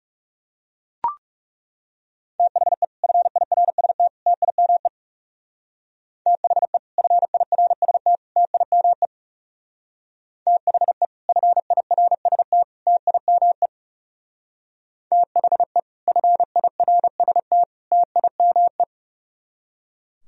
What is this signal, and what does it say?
Morse code, heard as a keyed tone, about 38 wpm, 700 Hz. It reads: THE FIRST TIME THE FIRST TIME THE FIRST TIME THE FIRST TIME